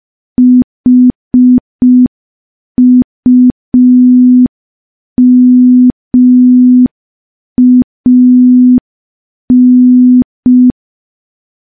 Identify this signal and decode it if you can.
Morse code, heard as a keyed tone, about 5 wpm, 250 Hz. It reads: HUMAN
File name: morse / morse-training-05WPM